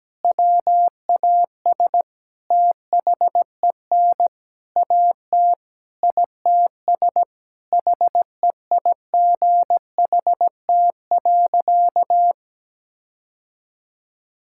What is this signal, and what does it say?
Morse code, heard as a keyed tone, about 17 wpm, 700 Hz. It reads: WAS THEN AT ITS HEIGHT.
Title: Sentences from Top 1600 Words 17wpm